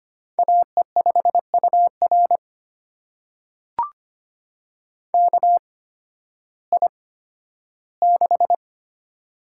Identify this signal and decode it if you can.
Morse code, heard as a keyed tone, about 25 wpm, spaced as for 13 wpm, 700 Hz. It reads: AE5UR KI6
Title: USA Call Signs - 5 Characters - 8x Character Spacing - 25wpm